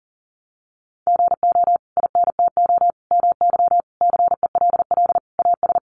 Morse code, heard as a keyed tone, about 40 wpm, 700 Hz. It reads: GO INTO MY CELL AS
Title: Sentences from Top 1100 Words 40wpm